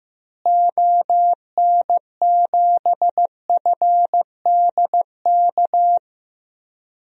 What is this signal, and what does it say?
Morse code, heard as a keyed tone, about 15 wpm, 700 Hz. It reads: ON7FDK
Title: DXCC-100 Call Signs - 6 Characters - Repeated 3x - 15wpm